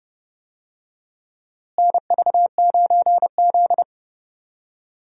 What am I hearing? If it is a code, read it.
Morse code, heard as a keyed tone, about 30 wpm, 700 Hz. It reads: NV9Z
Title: USA Call Signs - 4 Characters - Repeated 2x - 30wpm